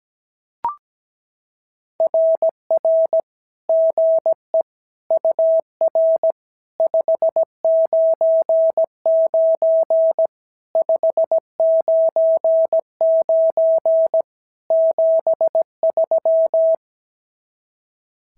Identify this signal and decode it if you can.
Morse code, heard as a keyed tone, about 16 wpm, 650 Hz. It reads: RR GE UR 599 599 73
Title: SOTA QSO Practice 17wpm